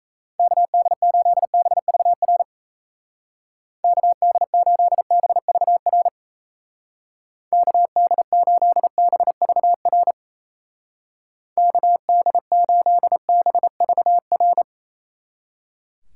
Morse code, heard as a keyed tone, about 36 wpm, 700 Hz. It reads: KD8BVR KD8BVR KD8BVR KD8BVR